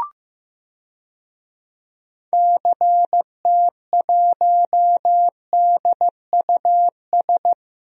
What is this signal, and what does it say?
Morse code, heard as a keyed tone, about 15 wpm, 700 Hz. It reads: CT1DUS
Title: DXCC-100 Call Signs - 6 Characters - Repeated 2x - 15wpm